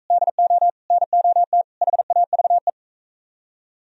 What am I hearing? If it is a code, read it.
Morse code, heard as a keyed tone, about 43 wpm, 700 Hz. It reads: DO NOT HAVE